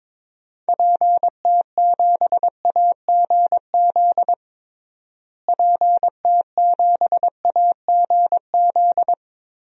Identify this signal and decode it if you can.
Morse code, heard as a keyed tone, about 22 wpm, 700 Hz. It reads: PT7AGZ PT7AGZ
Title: DXCC-100 Call Signs - 6 Characters - Repeated 3x - 22wpm